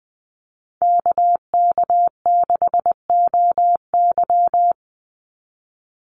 Morse code, heard as a keyed tone, about 20 wpm, 700 Hz. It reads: KK6OY
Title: USA Call Signs 20wpm